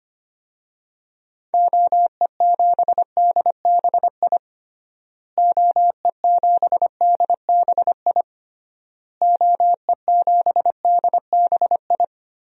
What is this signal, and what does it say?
Morse code, heard as a keyed tone, about 25 wpm, 700 Hz. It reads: OE7DBI OE7DBI OE7DBI
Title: DXCC-100 Call Signs - 6 Characters - Repeated 3x - 25wpm